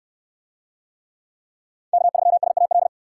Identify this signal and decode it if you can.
Morse code, heard as a keyed tone, about 64 wpm, 700 Hz. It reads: K3SAL